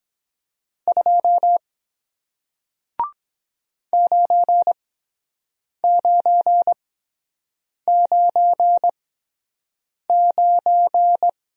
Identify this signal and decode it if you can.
Morse code, heard as a keyed tone, about 23 wpm, spaced as for 7 wpm, 700 Hz. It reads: 2 9999